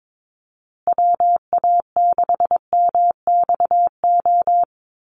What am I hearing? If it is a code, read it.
Morse code, heard as a keyed tone, about 22 wpm, 700 Hz. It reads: WA6MXO
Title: USA Call Signs Repeated 2x 22wpm